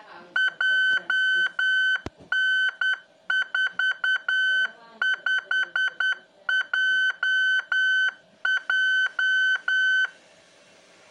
Morse code, heard as a keyed tone, about 10 wpm, 1500 Hz. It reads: JN45JJ
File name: ik1hgi / utiliti / ik1hgi-qrss